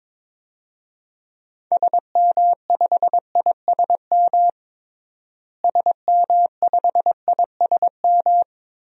Morse code, heard as a keyed tone, about 22 wpm, 700 Hz. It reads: SM5ISM SM5ISM
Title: DXCC-100 Call Signs - 6 Characters - Repeated 3x - 22wpm